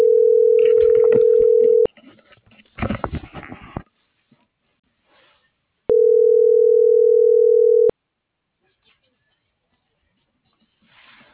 Ringback tone (North America), 440 + 480 Hz.